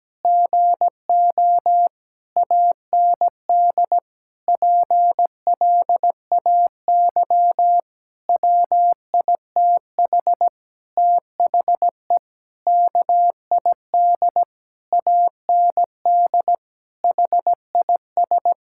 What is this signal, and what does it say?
Morse code, heard as a keyed tone, about 17 wpm, 700 Hz. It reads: GO AND PLAY WITH THE KID AND HIS
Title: Sentences from Top 1600 Words 17wpm